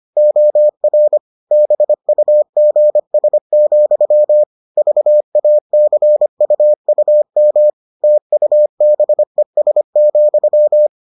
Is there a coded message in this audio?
Yes: Morse code from 0.2 to 10.9 s, heard as a keyed tone, about 25 wpm, 600 Hz. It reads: OR BUGS, VACUUM TUBES,